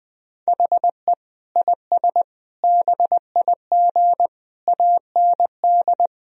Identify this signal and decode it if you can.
Morse code, heard as a keyed tone, about 20 wpm, 700 Hz. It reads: HE IS BIG AND
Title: Sentences from Top 1000 Words - Review Part 3 - 20wpm